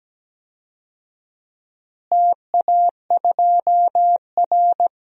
Morse code, heard as a keyed tone, about 17 wpm, 700 Hz. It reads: TA2R
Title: DXCC-100 Call Signs - 4 Characters - Repeated 3x - 17wpm